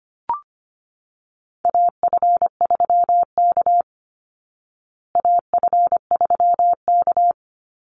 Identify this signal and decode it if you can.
Morse code, heard as a keyed tone, about 25 wpm, 700 Hz. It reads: AF3K AF3K
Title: Call Signs Repeated 2x - 25wpm